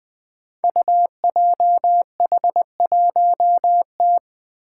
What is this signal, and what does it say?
Morse code, heard as a keyed tone, about 21 wpm, 700 Hz. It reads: UJH1T
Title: Five Challenging Letter-Number Combinations 20wpm